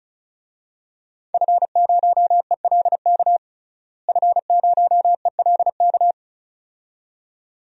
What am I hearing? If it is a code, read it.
Morse code, heard as a keyed tone, about 36 wpm, 700 Hz. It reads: F0ELK F0ELK
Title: DXCC-100 Call Signs - 5 Characters - Repeated 2x - 35wpm